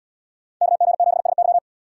Morse code, heard as a keyed tone, about 68 wpm, 700 Hz. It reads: KG7SY